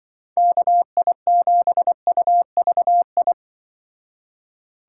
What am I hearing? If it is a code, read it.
Morse code, heard as a keyed tone, about 25 wpm, 700 Hz. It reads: KI7UVI